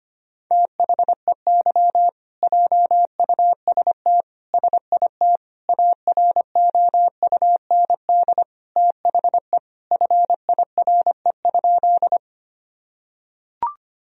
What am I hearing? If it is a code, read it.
Morse code, heard as a keyed tone, about 25 wpm, 700 Hz. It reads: THEY JUST SIT AROUND THE FIRE?
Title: Sentences from Top 1000 Words - Review Part 3 - 25wpm